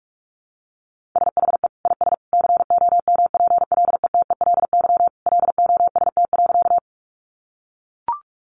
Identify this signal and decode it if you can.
Morse code, heard as a keyed tone, about 44 wpm, 700 Hz. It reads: SHE IS COMPLETELY LOST.
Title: Additional Easy Phrases 45wpm